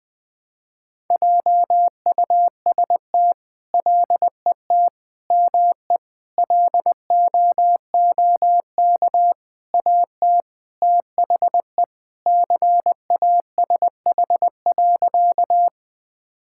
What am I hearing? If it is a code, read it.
Morse code, heard as a keyed tone, about 20 wpm, 700 Hz. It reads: JUST LET ME LOOK AT THE CASH.
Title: Sentences from Top 1000 Words - Review Part 5 - 20wpm